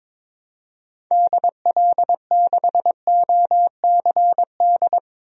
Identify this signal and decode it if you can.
Morse code, heard as a keyed tone, about 22 wpm, 700 Hz. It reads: DL6OCD
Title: DXCC-100 Call Signs - 6 Characters - Repeated 3x - 22wpm